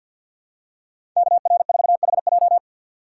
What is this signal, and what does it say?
Morse code, heard as a keyed tone, about 49 wpm, 700 Hz. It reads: KR4HJ
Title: USA Call Signs - 5 Characters - 3x Character Spacing - 50wpm